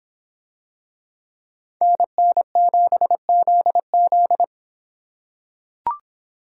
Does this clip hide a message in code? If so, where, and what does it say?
Yes: Morse code from 1.8 to 4.4 s, heard as a keyed tone, about 23 wpm, 700 Hz. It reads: NN7ZZ